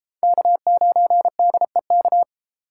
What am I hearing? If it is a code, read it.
Morse code, heard as a keyed tone, about 31 wpm, 700 Hz. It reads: K9DEK